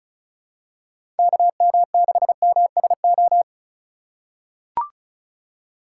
Morse code, heard as a keyed tone, about 34 wpm, 700 Hz. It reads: KM6MSO
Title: USA Call Signs - 6 Characters - 3x Character Spacing - 35wpm